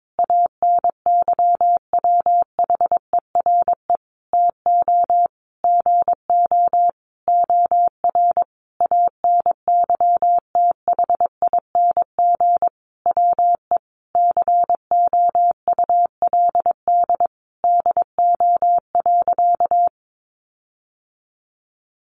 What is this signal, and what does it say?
Morse code, heard as a keyed tone, about 22 wpm, 700 Hz. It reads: ANYWHERE TO GO OR ANYTHING WE COULD DO.
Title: Sentences from Top 1400 Words 22wpm